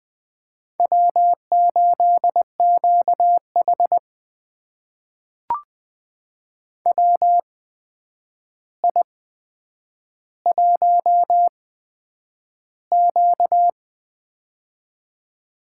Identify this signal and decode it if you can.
Morse code, heard as a keyed tone, about 20 wpm, spaced as for 9 wpm, 700 Hz. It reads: W8QH WI1Q
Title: USA Call Signs - 4 Characters - 8x Character Spacing - 20wpm